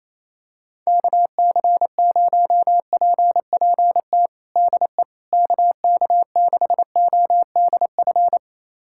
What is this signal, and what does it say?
Morse code, heard as a keyed tone, about 27 wpm, 700 Hz. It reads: KC0PPT DE KK6ODF
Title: USA Callsign DE Callsign 28wpm